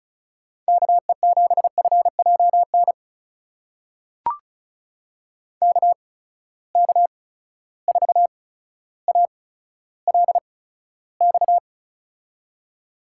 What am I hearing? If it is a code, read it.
Morse code, heard as a keyed tone, about 36 wpm, spaced as for 15 wpm, 700 Hz. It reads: KE7FJN KK4ALX
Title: USA Call Signs - 6 Characters - 8x Character Spacing - 35wpm